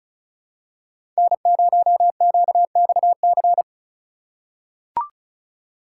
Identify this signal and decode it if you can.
Morse code, heard as a keyed tone, about 34 wpm, 700 Hz. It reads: N0QXC
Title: USA Call Signs - 6x Character Spacing - 35wpm